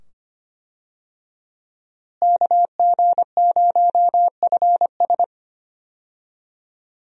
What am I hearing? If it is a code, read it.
Morse code, heard as a keyed tone, about 25 wpm, 700 Hz. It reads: KG0FS
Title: USA Call Signs - 5 Characters - Repeated 2x - 25wpm